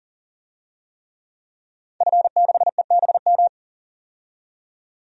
Morse code, heard as a keyed tone, about 40 wpm, 700 Hz. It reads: F6EBM
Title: DXCC-100 Call Signs - 5 Characters - 40wpm